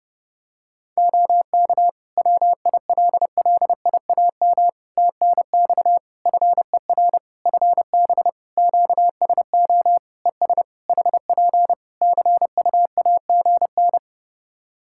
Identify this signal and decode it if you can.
Morse code, heard as a keyed tone, about 30 wpm, 700 Hz. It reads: OK WILLIAM TNX FER FB QSO ES HP CUAGN
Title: Practice QSO - Exchange 4 - Ending - 30wpm